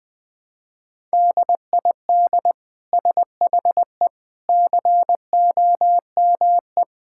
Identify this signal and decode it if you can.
Morse code, heard as a keyed tone, about 20 wpm, 700 Hz. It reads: DID SHE COME
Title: Sentences from Top 1000 Words - Review Part 5 - 20wpm